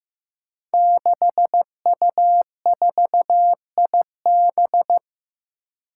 Morse code, heard as a keyed tone, about 15 wpm, 700 Hz. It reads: 6U4IB
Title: Five Challenging Letter-Number Combinations 15wpm